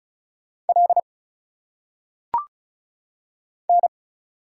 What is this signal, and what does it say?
Morse code, heard as a keyed tone, about 36 wpm, 700 Hz. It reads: L N
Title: Single Letters - Phonetic Pronunciation 35wpm